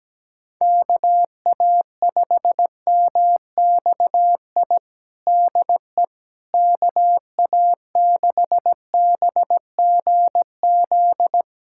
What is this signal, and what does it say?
Morse code, heard as a keyed tone, about 17 wpm, 700 Hz. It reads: KA5MXI DE KA6BGZ
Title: USA Callsign DE Callsign - 6 Characters - 17wpm